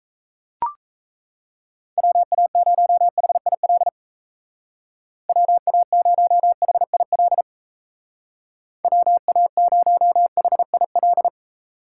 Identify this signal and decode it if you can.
Morse code, heard as a keyed tone, about 35 wpm, 700 Hz. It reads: WA0HIL WA0HIL WA0HIL
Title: Speed Racing - USA Call Signs 28wpm